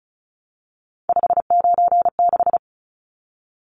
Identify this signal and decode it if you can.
Morse code, heard as a keyed tone, about 34 wpm, 700 Hz. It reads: 596
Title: Numbers - 3-Digit 35wpm